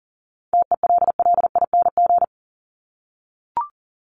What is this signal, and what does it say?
Morse code, heard as a keyed tone, about 40 wpm, 700 Hz. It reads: TELLING
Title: ING Suffix - Top 100 Words with Base Words 40wpm